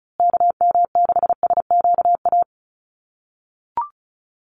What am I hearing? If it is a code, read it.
Morse code, heard as a keyed tone, about 34 wpm, 700 Hz. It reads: KM6SQA
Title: USA Call Signs Repeated 2x 35wpm